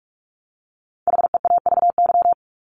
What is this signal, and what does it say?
Morse code, heard as a keyed tone, about 44 wpm, 700 Hz. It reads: HEAVY
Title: Top 1001-1250 Words 45wpm